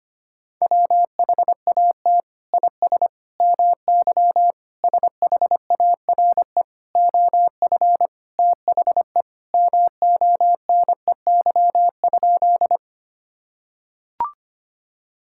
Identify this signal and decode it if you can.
Morse code, heard as a keyed tone, about 25 wpm, 700 Hz. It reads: WHAT IS MY SHARE OF THE MONEY?
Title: Sentences from Top 1000 Words - Review Part 6 - 25wpm